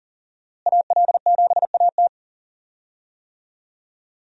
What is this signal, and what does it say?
Morse code, heard as a keyed tone, about 37 wpm, 700 Hz. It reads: AL7AT
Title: USA Call Signs - 5 Characters - Repeated 3x - 40wpm